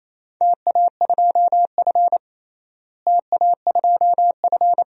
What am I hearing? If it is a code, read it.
Morse code, heard as a keyed tone, about 28 wpm, 700 Hz. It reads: TA2F TA2F
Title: DXCC-100 Call Signs - 4 Characters - Repeated 2x - 28wpm